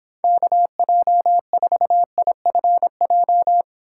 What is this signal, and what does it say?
Morse code, heard as a keyed tone, about 26 wpm, 700 Hz. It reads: KJ4IFJ